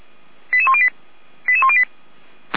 Five-tone selective call (ZVEI), heard as quick runs of five tones, tones of about 70 ms. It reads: 88108 88108